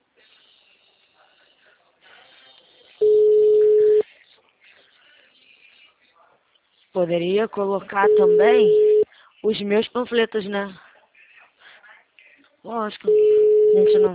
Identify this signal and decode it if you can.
Ringback tone (Europe, 425 Hz), with speech.